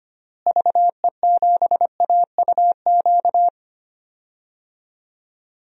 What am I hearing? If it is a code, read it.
Morse code, heard as a keyed tone, about 25 wpm, 700 Hz. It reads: VE7AUQ
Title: DXCC-100 Call Signs - 6 Characters - Repeated 3x - 25wpm